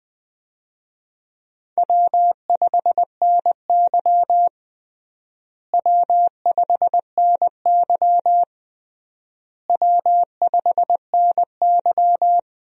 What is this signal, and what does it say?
Morse code, heard as a keyed tone, about 20 wpm, 700 Hz. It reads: W5NY W5NY W5NY
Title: USA Call Signs - 4 Characters - Repeated 3x - 20wpm